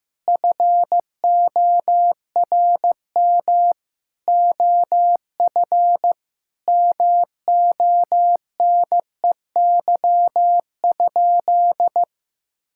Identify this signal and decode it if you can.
Morse code, heard as a keyed tone, about 15 wpm, 700 Hz. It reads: FORM OF MONEY?
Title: Sentences from Top 1000 Words - Review Part 6 - 15wpm